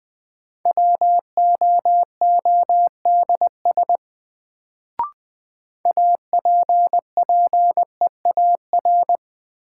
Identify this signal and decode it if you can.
Morse code, heard as a keyed tone, about 20 wpm, 700 Hz. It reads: WOODS APPEAR
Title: Sentences from Top 1000 Words - Review Part 2 - New Word Drill - 20wpm